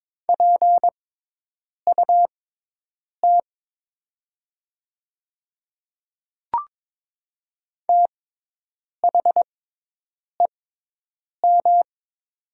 Morse code, heard as a keyed tone, about 22 wpm, spaced as for 8 wpm, 700 Hz. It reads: PUT THEM
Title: 6x Character Spacing - Top 300 Words - 22wpm